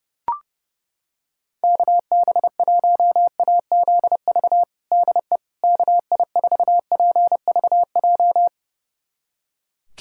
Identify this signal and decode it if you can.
Morse code, heard as a keyed tone, about 32 wpm, 700 Hz. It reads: KB1AZV DE KI4PVJ
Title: USA Callsign DE Callsign - 6 Characters - 30wpm